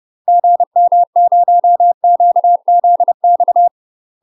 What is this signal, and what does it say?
Morse code, heard as a keyed tone, about 30 wpm, 700 Hz. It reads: GM0QZX
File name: UK MASTER QSOs TRAINING / 30WPM / UK MASTER QSOs 0039